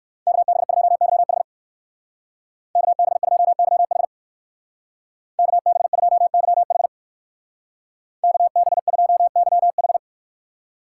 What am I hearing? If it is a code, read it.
Morse code, heard as a keyed tone, about 56 wpm, 700 Hz. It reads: KB2YH KB2YH KB2YH KB2YH